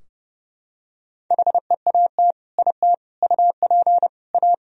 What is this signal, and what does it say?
Morse code, heard as a keyed tone, about 30 wpm, 700 Hz. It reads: HEAT IT UP A
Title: Sentences from Top 1000 Words - Review Part 3 - 30wpm